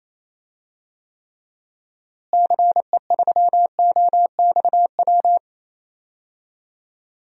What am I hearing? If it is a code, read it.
Morse code, heard as a keyed tone, about 28 wpm, 700 Hz. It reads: CE3OXW
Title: DXCC-100 Call Signs - 6 Characters - Repeated 3x - 28wpm